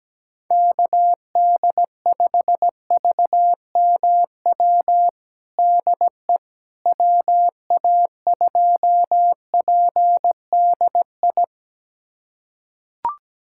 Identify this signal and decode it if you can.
Morse code, heard as a keyed tone, about 17 wpm, 700 Hz. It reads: KD5VMW DE WA2PDI
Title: USA Callsign DE Callsign - 6 Characters - 17wpm